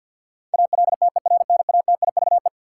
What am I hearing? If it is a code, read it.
Morse code, heard as a keyed tone, about 51 wpm, 700 Hz. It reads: ALTERNATIVE
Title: Sentences from Top 1000 Words - Review Part 4 - New Word Drill - 50wpm